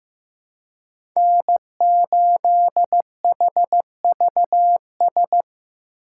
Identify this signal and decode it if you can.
Morse code, heard as a keyed tone, about 15 wpm, 700 Hz. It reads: N8HVS
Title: USA Call Signs - 5 Characters - 4x Character Spacing - 15wpm